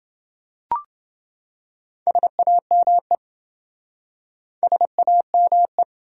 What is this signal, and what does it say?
Morse code, heard as a keyed tone, about 28 wpm, 700 Hz. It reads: SAME SAME